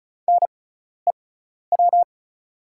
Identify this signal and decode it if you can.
Morse code, heard as a keyed tone, about 34 wpm, spaced as for 12 wpm, 700 Hz. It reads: NEW
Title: 6x Character Spacing - Top 300 Words - 35wpm